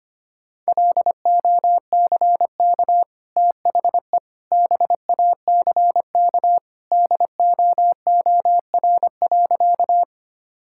Morse code, heard as a keyed tone, about 25 wpm, 700 Hz. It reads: LOCK THE BACK DOOR.
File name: Easy Sentences with Hints - 25wpm